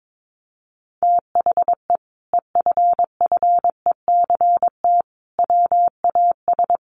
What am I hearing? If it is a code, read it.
Morse code, heard as a keyed tone, about 22 wpm, 700 Hz. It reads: THE EFFECT WAS
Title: Sentences from Top 1200 Words 22wpm